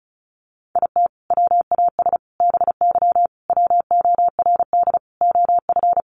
Morse code, heard as a keyed tone, about 34 wpm, 700 Hz. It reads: IT WAS BY WORD OF
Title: Sentences from Top 1500 Words 35wpm